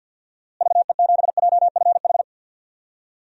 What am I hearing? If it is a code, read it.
Morse code, heard as a keyed tone, about 47 wpm, 700 Hz. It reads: VE7JFH
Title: DXCC-100 Call Signs - 6 Characters - Repeated 2x - 50wpm